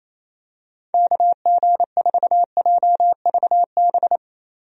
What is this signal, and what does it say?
Morse code, heard as a keyed tone, about 27 wpm, 700 Hz. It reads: KG4JVB
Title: USA Call Signs - 6 Characters - Repeated 2x - 28wpm